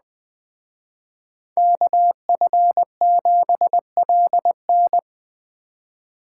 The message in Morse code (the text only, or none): KF7LN